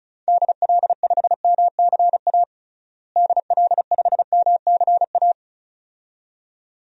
Morse code, heard as a keyed tone, about 36 wpm, 700 Hz. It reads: DL5MCA DL5MCA
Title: DXCC-100 Call Signs - 6 Characters - Repeated 2x - 35wpm